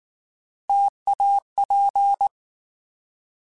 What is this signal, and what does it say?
Morse code, heard as a keyed tone, about 19 wpm, 800 Hz. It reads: TAP
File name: Alphabetical / Alphabetical mp3s / Ts @18